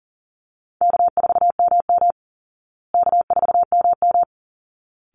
Morse code, heard as a keyed tone, about 40 wpm, 700 Hz. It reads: K4MM K4MM
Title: Call Signs Repeated 3x - 40wpm